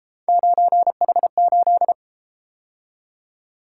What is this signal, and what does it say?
Morse code, heard as a keyed tone, about 31 wpm, 700 Hz. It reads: 9H8